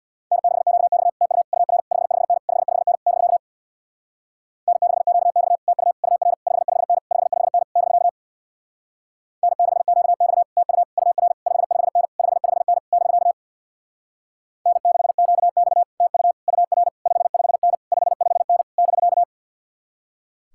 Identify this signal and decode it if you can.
Morse code, heard as a keyed tone, about 64 wpm, 700 Hz. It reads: N6QX TU UR 55N 55N <BK> N6QX TU UR 55N 55N <BK> N6QX TU UR 55N 55N <BK> N6QX TU UR 55N 55N <BK>